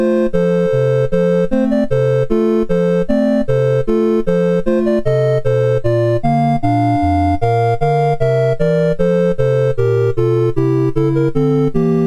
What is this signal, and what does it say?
There is no code.